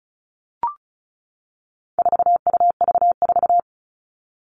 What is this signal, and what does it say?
Morse code, heard as a keyed tone, about 34 wpm, 700 Hz. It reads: 4UV4
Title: All Cut Numbers - 4-Digit 35wpm